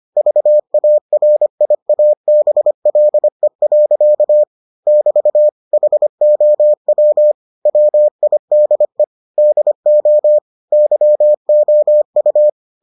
Morse code, heard as a keyed tone, about 25 wpm, 600 Hz. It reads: VARIABLE. <BT> HOW WIDE DO YOU